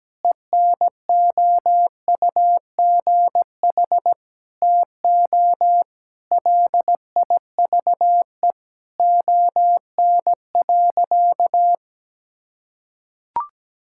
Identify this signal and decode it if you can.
Morse code, heard as a keyed tone, about 17 wpm, 700 Hz. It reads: ENOUGH TO LIVE ON.
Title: Sentences from Top 1000 Words - Review Part 6 - 17wpm